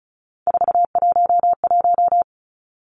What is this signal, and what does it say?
Morse code, heard as a keyed tone, about 36 wpm, 700 Hz. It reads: 411
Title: Three Challenging Letter-Number Combinations 35wpm